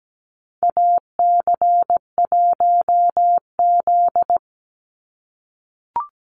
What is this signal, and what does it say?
Morse code, heard as a keyed tone, about 17 wpm, 700 Hz. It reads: AC1Z
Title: NA SOTA Activator Callsigns 17wpm